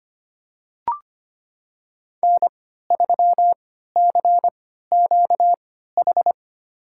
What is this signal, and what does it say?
Morse code, heard as a keyed tone, about 25 wpm, spaced as for 14 wpm, 700 Hz. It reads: N3CQH
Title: USA Call Signs - 5 Characters - 3x Character Spacing - 25wpm